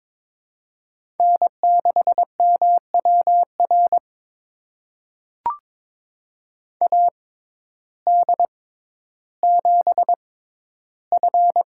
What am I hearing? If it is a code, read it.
Morse code, heard as a keyed tone, about 22 wpm, spaced as for 12 wpm, 700 Hz. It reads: N6MWR AD7F
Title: USA Call Signs - 5 Characters - 6x Character Spacing - 22wpm